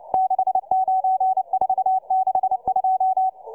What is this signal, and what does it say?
Morse code, heard as a keyed tone, about 29 wpm, 760 Hz, with radio noise. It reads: 69462